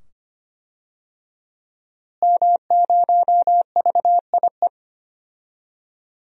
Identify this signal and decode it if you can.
Morse code, heard as a keyed tone, about 25 wpm, 700 Hz. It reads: M0VIE
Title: DXCC-100 Call Signs - 5 Characters - Repeated 2x - 25wpm